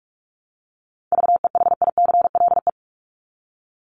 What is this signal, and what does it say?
Morse code, heard as a keyed tone, about 44 wpm, 700 Hz. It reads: VEHICLE
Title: Top 1751-2000 Words 45wpm